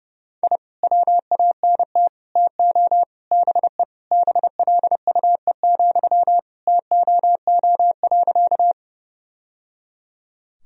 Morse code, heard as a keyed tone, about 30 wpm, 700 Hz. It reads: I WANT TO BE BLUE, TOO.
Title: Sentences from Top 1000 Words - Review Part 3 - 30wpm